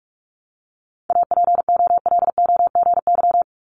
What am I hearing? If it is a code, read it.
Morse code, heard as a keyed tone, about 46 wpm, 700 Hz. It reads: APOLOGY
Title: Multi-letter Phonemes Repeated 3x 45wpm